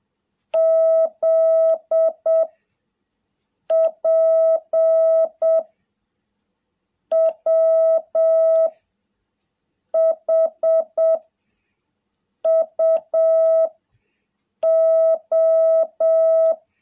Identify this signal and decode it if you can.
Morse code, heard as a keyed tone, about 7 wpm, 660 Hz. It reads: Z P W H U O
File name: jh8cbh / 2021cw